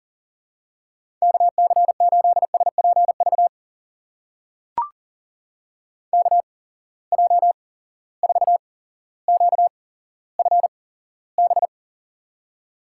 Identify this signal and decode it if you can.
Morse code, heard as a keyed tone, about 42 wpm, spaced as for 17 wpm, 700 Hz. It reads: KC8SPV KJ4QFB
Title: USA Call Signs - 6 Characters - 8x Character Spacing - 40wpm